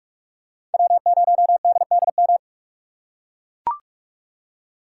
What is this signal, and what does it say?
Morse code, heard as a keyed tone, about 41 wpm, 700 Hz. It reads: W0DDM